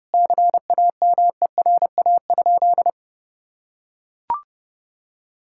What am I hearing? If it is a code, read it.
Morse code, heard as a keyed tone, about 30 wpm, 700 Hz. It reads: CAMERA?
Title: Sentences from Top 1000 Words - Review Part 5 - 30wpm